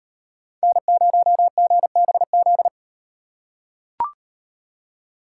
Morse code, heard as a keyed tone, about 38 wpm, 700 Hz. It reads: N0GBZ